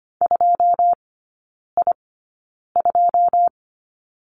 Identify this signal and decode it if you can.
Morse code, heard as a keyed tone, about 25 wpm, spaced as for 9 wpm, 700 Hz. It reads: 2I2
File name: Lesson 14e - Number 2 - Three Characters 25wpm